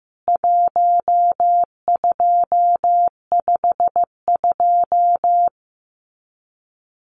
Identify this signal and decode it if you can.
Morse code, heard as a keyed tone, about 14 wpm, 700 Hz. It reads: 1252
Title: CWT Exchange 15wpm